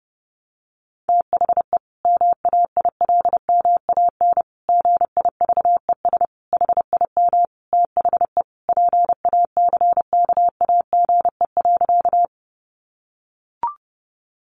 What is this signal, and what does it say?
Morse code, heard as a keyed tone, about 30 wpm, 700 Hz. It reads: THE MAILMAN GIVES HIM THE PACKAGE.
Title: Easy Phrases 30wpm